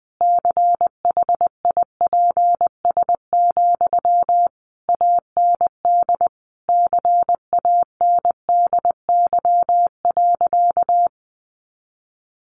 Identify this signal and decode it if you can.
Morse code, heard as a keyed tone, about 20 wpm, 700 Hz. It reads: CHIPS, AND CANDY.
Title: Easy Phrases 20wpm